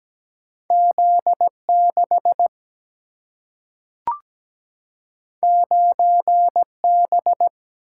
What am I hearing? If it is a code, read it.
Morse code, heard as a keyed tone, about 17 wpm, 700 Hz. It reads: Z6 9B
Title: Single-Letter Single-Number 17wpm